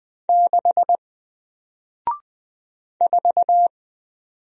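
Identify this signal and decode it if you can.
Morse code, heard as a keyed tone, about 20 wpm, 700 Hz. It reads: 6 4
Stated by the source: Single letter and number Morse drill at 20 wpm, from N4LCD.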